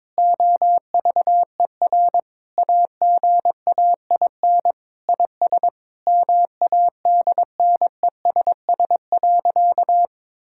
Morse code, heard as a keyed tone, about 22 wpm, 700 Hz. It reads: OVER AGAIN IS MADNESS.
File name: Sentences with 7-Word Phrases - 22wpm